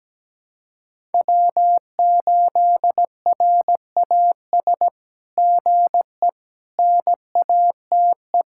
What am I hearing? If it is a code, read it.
Morse code, heard as a keyed tone, about 17 wpm, 700 Hz. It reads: W8RAS GE NATE
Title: SOTA QSO Practice 17wpm